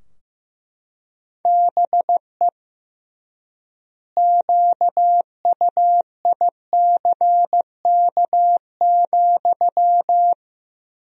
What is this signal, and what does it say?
Morse code, heard as a keyed tone, about 15 wpm, 700 Hz. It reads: BE QUICK,
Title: 3x Word-Spacing - Sentences from Top 500 Words - 15wpm